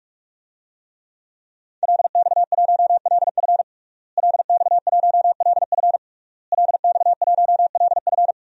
Morse code, heard as a keyed tone, about 47 wpm, 700 Hz. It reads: LX1LF LX1LF LX1LF
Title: DXCC-100 Call Signs - 5 Characters - Repeated 3x - 45wpm